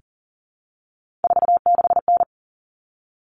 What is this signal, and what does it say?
Morse code, heard as a keyed tone, about 40 wpm, 700 Hz. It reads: 46N
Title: All Cut Numbers - 3-Digit 40wpm